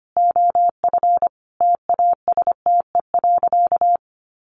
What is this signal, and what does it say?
Morse code, heard as a keyed tone, about 25 wpm, 700 Hz. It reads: OF TASTE.
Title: Sentences from Top 1100 Words 25wpm